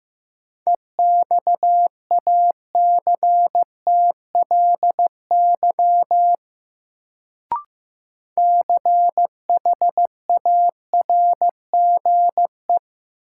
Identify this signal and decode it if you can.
Morse code, heard as a keyed tone, about 15 wpm, 700 Hz. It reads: EXACTLY CHARGE
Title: Sentences from Top 900 Words - New Word Drill - 15wpm